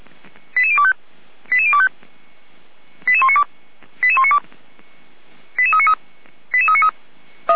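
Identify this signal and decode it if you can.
Five-tone selective call (ZVEI), heard as quick runs of five tones, tones of about 70 ms. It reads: 88025 88025 88192 88192 88393 88393
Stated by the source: ZVEI five-tone selective call received by SDR, recorded by Feuerwehr Orlamünde.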